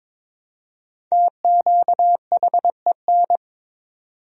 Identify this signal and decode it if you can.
Morse code, heard as a keyed tone, about 21 wpm, 700 Hz. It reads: TQHEN
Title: 4x Word-Spacing - Sentences from Top 100 Words - 22wpm